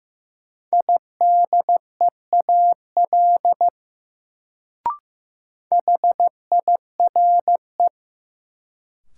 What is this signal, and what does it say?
Morse code, heard as a keyed tone, about 15 wpm, 700 Hz. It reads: IDEAL HIRE
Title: Sentences from Top 1400 Words - New Word Drill Alt - 15wpm